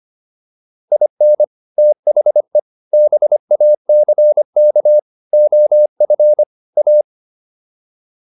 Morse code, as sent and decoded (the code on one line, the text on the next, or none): .. -. / - .... / -... .- -.-. -.- / --- ..-. / .-
IN THE BACK OF A